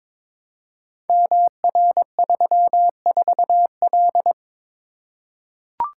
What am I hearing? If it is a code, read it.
Morse code, heard as a keyed tone, about 23 wpm, 700 Hz. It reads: MR34L
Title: Code Groups - 5 Letter:Numbers - 22wpm